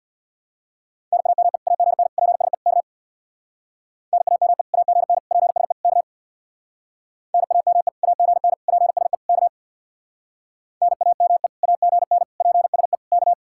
Morse code, heard as a keyed tone, about 64 wpm, 700 Hz. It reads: NAME AGN PSE K NAME AGN PSE K NAME AGN PSE K NAME AGN PSE K